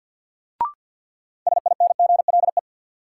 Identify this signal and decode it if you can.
Morse code, heard as a keyed tone, about 53 wpm, 700 Hz. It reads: SINGLE
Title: Sentences from Top 700 Words - New Word Drill - 50wpm